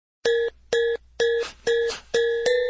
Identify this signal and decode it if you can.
There is no code.